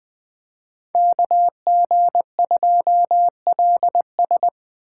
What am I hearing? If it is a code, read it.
Morse code, heard as a keyed tone, about 24 wpm, 700 Hz. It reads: KG2LS
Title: Speed Racing - USA 2x2 Call Signs 20wpm